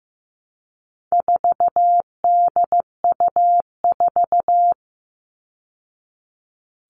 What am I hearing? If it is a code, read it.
Morse code, heard as a keyed tone, about 15 wpm, 700 Hz. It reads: 4DU4
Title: All Cut Numbers - 4-Digit 15wpm